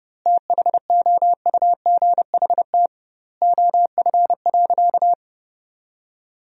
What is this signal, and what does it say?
Morse code, heard as a keyed tone, about 30 wpm, 700 Hz. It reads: THOUGHT OF.
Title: 2x Word-Spacing - Sentences from Top 300 Words - 30wpm